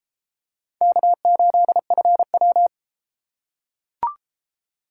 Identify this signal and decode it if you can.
Morse code, heard as a keyed tone, about 33 wpm, 700 Hz. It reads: K8FW